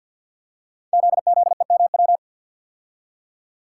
Morse code, heard as a keyed tone, about 46 wpm, 700 Hz. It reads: ZZEMW
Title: Code Groups - 5 Letters - 50wpm